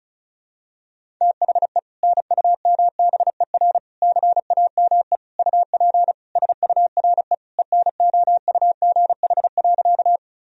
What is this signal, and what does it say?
Morse code, heard as a keyed tone, about 34 wpm, 700 Hz. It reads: THE NUMBER CAME UP SURE ENOUGH.
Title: Sentences from Top 1000 Words - Review Part 1 - 35wpm